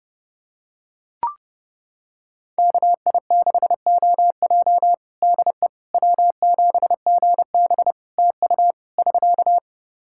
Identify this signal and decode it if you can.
Morse code, heard as a keyed tone, about 32 wpm, 700 Hz. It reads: KI6OJ DE W7GB TU <SK>
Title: Practice QSO - Exchange 4 - Ending - 30wpm